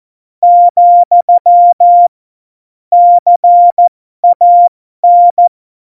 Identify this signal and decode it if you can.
Morse code, heard as a keyed tone, about 14 wpm, spaced as for 12 wpm, 700 Hz. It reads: , CAN